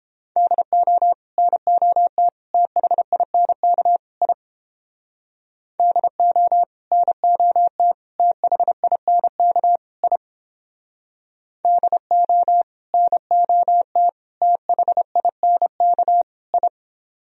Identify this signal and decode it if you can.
Morse code, heard as a keyed tone, about 29 wpm, 700 Hz. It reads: DO NOT THINK I DO NOT THINK I DO NOT THINK I